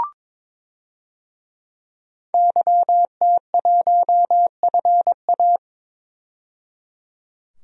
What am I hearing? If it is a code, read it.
Morse code, heard as a keyed tone, about 22 wpm, 700 Hz. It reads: YT1FA
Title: DXCC-100 Call Signs - 5 Characters - 22wpm